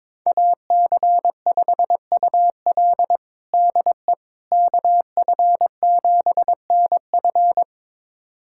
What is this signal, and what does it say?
Morse code, heard as a keyed tone, about 22 wpm, 700 Hz. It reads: AC5UL DE KF7NF
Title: USA Callsign DE Callsign 22wpm